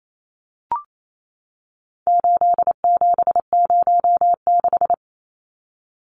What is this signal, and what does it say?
Morse code, heard as a keyed tone, about 28 wpm, 700 Hz. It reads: 8706
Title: Numbers - 4-Digit 28wpm